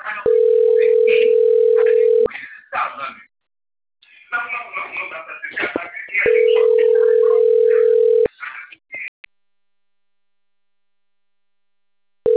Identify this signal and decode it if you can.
Ringback tone (North America), 440 + 480 Hz, with speech.
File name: rg-1600-unknown- - - .21127